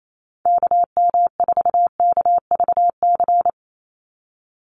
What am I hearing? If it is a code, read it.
Morse code, heard as a keyed tone, about 28 wpm, 700 Hz. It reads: KM4KVC
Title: USA Call Signs 28wpm